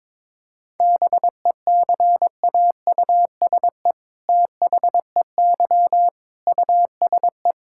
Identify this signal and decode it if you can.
Morse code, heard as a keyed tone, about 22 wpm, 700 Hz. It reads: BECAUSE THEY USE
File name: All-Age Appropriate Jokes - 22wpm